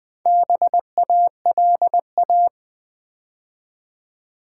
Morse code, heard as a keyed tone, about 20 wpm, 700 Hz. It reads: BALA
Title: Build A Word - 4 Letters At A Time - 8-Letter Words - 20wpm